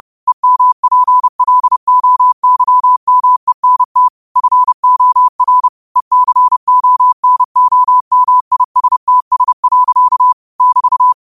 Morse code, heard as a keyed tone, about 30 wpm, 1000 Hz. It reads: EMPLOYMENT FOR ECONOMISTS. <BT>